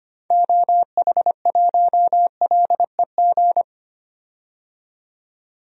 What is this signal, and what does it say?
Morse code, heard as a keyed tone, about 25 wpm, 700 Hz. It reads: OH1LEG
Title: DXCC-100 Call Signs - 6 Characters - 25wpm